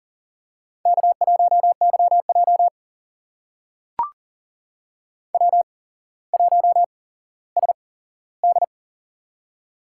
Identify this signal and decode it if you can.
Morse code, heard as a keyed tone, about 39 wpm, spaced as for 17 wpm, 700 Hz. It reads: K1YJ W1SD
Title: USA Call Signs - 4 Characters - 8x Character Spacing - 40wpm